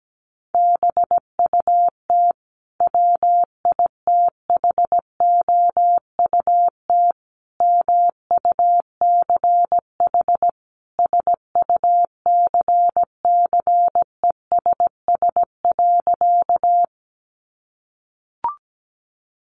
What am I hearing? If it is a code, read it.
Morse code, heard as a keyed tone, about 17 wpm, 700 Hz. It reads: BUT WITHOUT MUCH SUCCESS.
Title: Sentences from Top 900 Words 17wpm